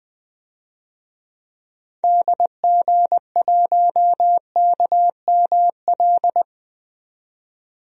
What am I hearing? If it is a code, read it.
Morse code, heard as a keyed tone, about 20 wpm, 700 Hz. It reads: DG1KML
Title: DXCC-100 Call Signs - 6 Characters - 20wpm